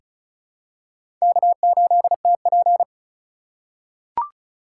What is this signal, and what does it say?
Morse code, heard as a keyed tone, about 36 wpm, 700 Hz. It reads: K8TP
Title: USA Call Signs - 4x Character Spacing - 35wpm